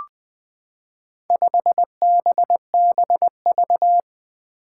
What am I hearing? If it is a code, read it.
Morse code, heard as a keyed tone, about 20 wpm, 700 Hz. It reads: 5BBV
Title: Four Challenging Letter-Number Combinations 20wpm